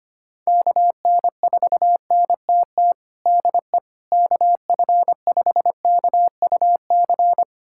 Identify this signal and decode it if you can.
Morse code, heard as a keyed tone, about 25 wpm, 700 Hz. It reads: KN4NTT DE KF5KUC
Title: USA Callsign DE Callsign 25wpm